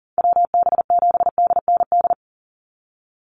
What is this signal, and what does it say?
Morse code, heard as a keyed tone, about 40 wpm, 700 Hz. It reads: WB7DND
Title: Call Signs Repeated 2x - 40wpm